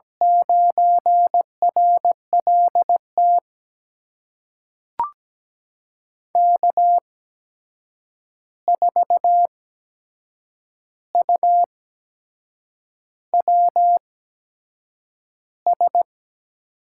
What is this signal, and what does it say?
Morse code, heard as a keyed tone, about 17 wpm, spaced as for 7 wpm, 700 Hz. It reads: 9RLT K4UWS
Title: USA Call Signs - 8x Character Spacing - 17wpm